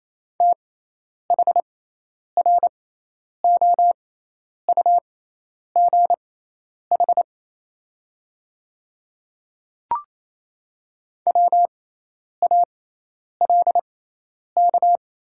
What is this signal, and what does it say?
Morse code, heard as a keyed tone, about 28 wpm, spaced as for 10 wpm, 700 Hz. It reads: THROUGH WALK
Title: 6x Character Spacing - Top 300 Words - 28wpm